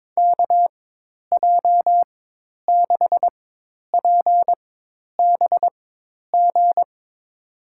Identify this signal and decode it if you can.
Morse code, heard as a keyed tone, about 22 wpm, spaced as for 10 wpm, 700 Hz. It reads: KJ6PBG
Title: USA Call Signs - 6 Characters - 4x Character Spacing - 22wpm